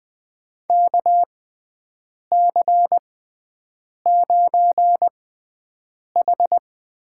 Morse code, heard as a keyed tone, about 20 wpm, spaced as for 7 wpm, 700 Hz. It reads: KC9H
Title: USA Call Signs - 6 Characters - 6x Character Spacing - 20wpm